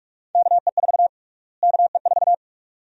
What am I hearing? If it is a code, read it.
Morse code, heard as a keyed tone, about 45 wpm, 700 Hz. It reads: KE4 KE4
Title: USA Call Sign Prefixes - Repeated 2x - 45wpm